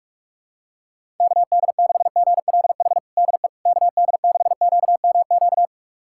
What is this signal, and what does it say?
Morse code, heard as a keyed tone, about 46 wpm, 700 Hz. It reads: KD6GLH DE KD6QMQ
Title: USA Callsign DE Callsign - 6 Characters - 45wpm